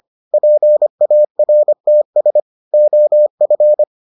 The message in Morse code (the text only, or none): PARTS OF